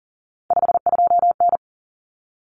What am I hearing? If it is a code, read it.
Morse code, heard as a keyed tone, about 40 wpm, 700 Hz. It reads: 52N
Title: Common Cut Numbers - 3-Digit 40wpm